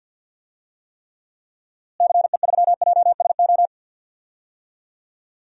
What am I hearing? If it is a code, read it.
Morse code, heard as a keyed tone, about 51 wpm, 700 Hz. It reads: CE3JSO
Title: DXCC-100 Call Signs - 6 Characters - Repeated 2x - 50wpm